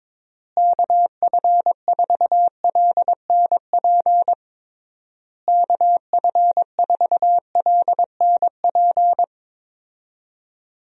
Morse code, heard as a keyed tone, about 22 wpm, 700 Hz. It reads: KF4LNP KF4LNP
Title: USA Call Signs - 6 Characters - Repeated 3x - 22wpm